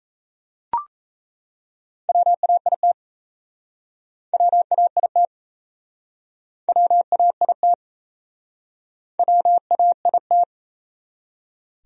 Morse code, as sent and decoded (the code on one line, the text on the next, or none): .-- .- .. - / .-- .- .. - / .-- .- .. - / .-- .- .. -
WAIT WAIT WAIT WAIT